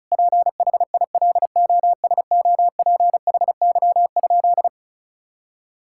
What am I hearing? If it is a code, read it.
Morse code, heard as a keyed tone, about 36 wpm, 700 Hz. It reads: PHILOSOPHY?
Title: Sentences from Top 1000 Words - Review Part 6 - 35wpm